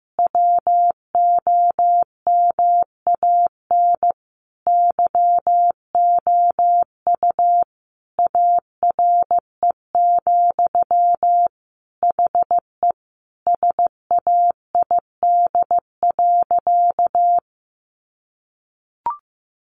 Morse code, heard as a keyed tone, about 15 wpm, 700 Hz. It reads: WOMAN YOU ARE, HE SAID.
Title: Sentences from Top 1100 Words 15wpm